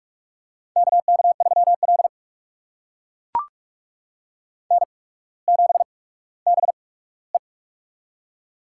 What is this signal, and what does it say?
Morse code, heard as a keyed tone, about 44 wpm, spaced as for 19 wpm, 700 Hz. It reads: KK3L N7BE
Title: USA Call Signs - 4 Characters - 8x Character Spacing - 45wpm